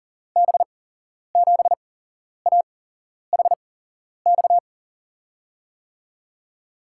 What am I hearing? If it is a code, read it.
Morse code, heard as a keyed tone, about 42 wpm, spaced as for 11 wpm, 700 Hz. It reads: B7AHX
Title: USA Call Signs - 8x Character Spacing - 40wpm